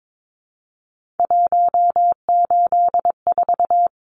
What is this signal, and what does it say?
Morse code, heard as a keyed tone, about 22 wpm, 700 Hz. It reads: 184
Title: Numbers - 3-Digit 22wpm